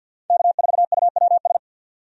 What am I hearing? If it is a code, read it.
Morse code, heard as a keyed tone, about 51 wpm, 700 Hz. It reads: K4UWS
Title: USA Call Signs - 5 Characters - 50wpm